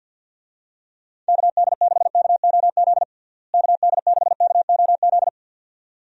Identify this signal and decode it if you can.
Morse code, heard as a keyed tone, about 48 wpm, 700 Hz. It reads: KD6KOZ KD6KOZ
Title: USA Call Signs - 6 Characters - Repeated 2x - 50wpm